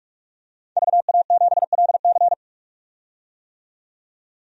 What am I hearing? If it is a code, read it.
Morse code, heard as a keyed tone, about 46 wpm, 700 Hz. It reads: VA7LC
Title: DXCC-100 Call Signs - 5 Characters - 45wpm